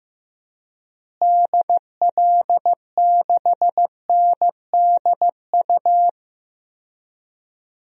Morse code, heard as a keyed tone, about 15 wpm, 700 Hz. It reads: DL6NDU
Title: DXCC-100 Call Signs - 6 Characters - Repeated 2x - 15wpm